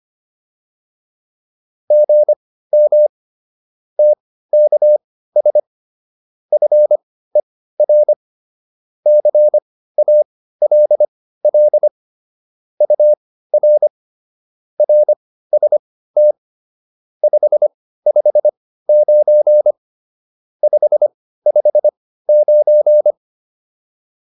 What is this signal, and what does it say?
Morse code, heard as a keyed tone, about 25 wpm, spaced as for 15 wpm, 600 Hz. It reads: GM TKS FER CALL UR RST 559 559
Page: intermediate-practice-files